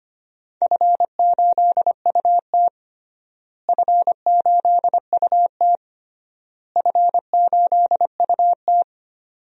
Morse code, heard as a keyed tone, about 25 wpm, 700 Hz. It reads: F8UT F8UT F8UT
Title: DXCC-100 Call Signs - 4 Characters - Repeated 3x - 25wpm